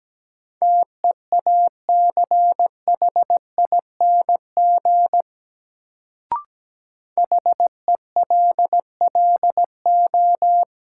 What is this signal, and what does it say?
Morse code, heard as a keyed tone, about 17 wpm, 700 Hz. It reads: TEACHING HELLO
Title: Sentences from Top 1100 Words - New Word Drill Alt - 17wpm